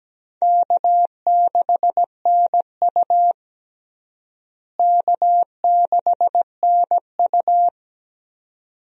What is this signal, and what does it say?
Morse code, heard as a keyed tone, about 17 wpm, 700 Hz. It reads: K6NU K6NU
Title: USA Call Signs - 4 Characters - Repeated 2x - 17wpm